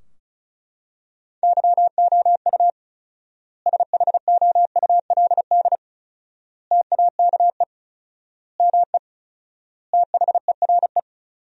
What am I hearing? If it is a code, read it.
Morse code, heard as a keyed tone, about 36 wpm, 700 Hz. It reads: YOU SHOULD TAKE ME THERE
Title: 4x Word-Spacing - Sentences from Top 300 Words - 35wpm